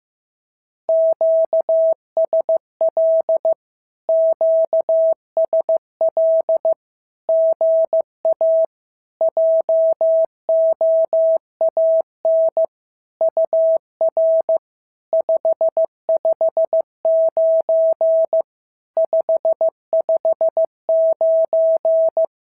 Morse code, heard as a keyed tone, about 15 wpm, 650 Hz. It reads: QSL QSL GA JOAN UR 559 559
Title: SOTA QSO Practice 15wpm